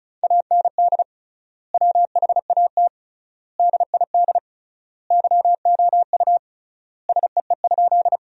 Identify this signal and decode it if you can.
Morse code, heard as a keyed tone, about 36 wpm, 700 Hz. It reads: AND WHAT DID YOU SEE?
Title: 3x Word-Spacing - Sentences from Top 100 Words - 35wpm